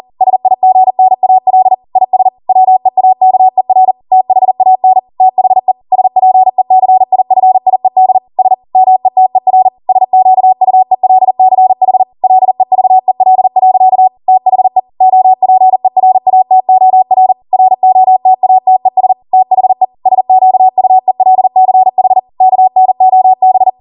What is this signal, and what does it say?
Morse code, heard as a keyed tone, about 40 wpm, 750 Hz. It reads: SIGNAL IS WEAKER THAN THE SPECIFIED S METER SQUELCH LEVEL. THE OPERATOR ROTATES THE SQUELCH KNOB
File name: Morse / 40WPM